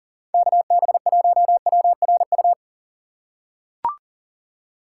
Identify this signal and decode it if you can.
Morse code, heard as a keyed tone, about 40 wpm, 700 Hz. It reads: KB1WRU
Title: USA Call Signs - 6 Characters - 3x Character Spacing - 40wpm